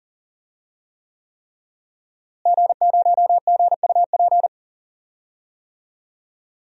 Morse code, heard as a keyed tone, about 40 wpm, 700 Hz. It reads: G0GUP
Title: DXCC-100 Call Signs - 5 Characters - Repeated 3x - 40wpm